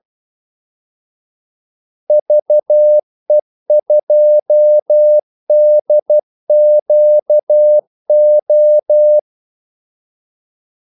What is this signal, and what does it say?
Morse code, heard as a keyed tone, about 12 wpm, 600 Hz. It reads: VE2DQO
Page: 2 ekstra ordmellemrum mellem kaldesignalerne